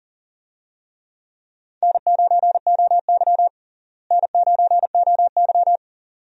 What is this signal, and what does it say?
Morse code, heard as a keyed tone, about 40 wpm, 700 Hz. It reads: N9OY N9OY
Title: USA Call Signs - 4 Characters - Repeated 2x - 40wpm